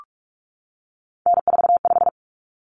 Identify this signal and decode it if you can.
Morse code, heard as a keyed tone, about 44 wpm, 700 Hz. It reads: N45
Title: Common Cut Numbers - 3-Digit 45wpm